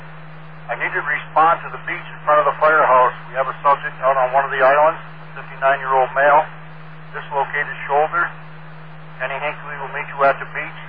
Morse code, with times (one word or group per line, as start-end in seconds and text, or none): none